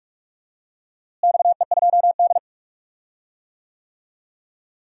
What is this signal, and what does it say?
Morse code, heard as a keyed tone, about 46 wpm, 700 Hz. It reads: XE2D
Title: DXCC-100 Call Signs - 4 Characters - Repeated 3x - 45wpm